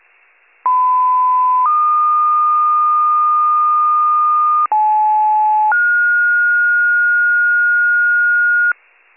Two-tone page, heard as a one-second tone, then a three-second tone, about 1010 then 1220 Hz, about 860 then 1430 Hz, with noise. It is heard from 0.6 to 8.7 s.